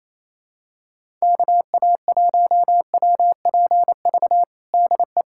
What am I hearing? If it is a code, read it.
Morse code, heard as a keyed tone, about 28 wpm, 700 Hz. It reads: KA1WPV DE
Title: USA Callsign DE Callsign - 6 Characters - 28wpm